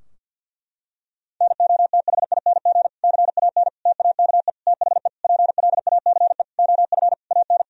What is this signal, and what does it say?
Morse code, heard as a keyed tone, about 50 wpm, 700 Hz. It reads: NOTHING CAN TAKE THE PLACE OF AN
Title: Sentences from Top 1000 Words - Review Part 4 - 50wpm